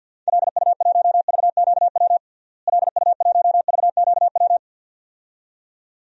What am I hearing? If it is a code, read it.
Morse code, heard as a keyed tone, about 52 wpm, 700 Hz. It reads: LU1VQW LU1VQW
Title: DXCC-100 Call Signs - 6 Characters - Repeated 2x - 50wpm